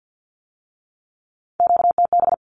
Morse code, heard as a keyed tone, about 50 wpm, 700 Hz. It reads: QTB
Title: Q-Codes with Spoken Meaning - 50wpm